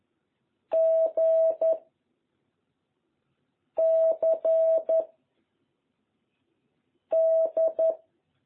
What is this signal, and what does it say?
Morse code, heard as a keyed tone, about 11 wpm, 660 Hz. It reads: G C D
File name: jh8cbh / 2021cw